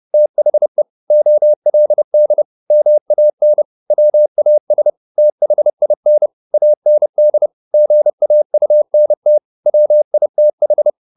Morse code, heard as a keyed tone, about 30 wpm, 600 Hz. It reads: THE OLD MAN WAS THIN AND GAUNT WITH
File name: the old man and the sea / the old man and the sea 1 30wpm